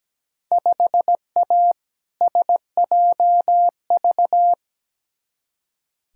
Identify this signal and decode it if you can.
Morse code, heard as a keyed tone, about 17 wpm, 700 Hz. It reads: 5A SJV
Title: Field Day Exchanges - 17wpm